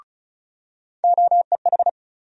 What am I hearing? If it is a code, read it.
Morse code, heard as a keyed tone, about 36 wpm, 700 Hz. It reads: OEH
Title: USA Call Sign Postfixes - 3 Letters 35wpm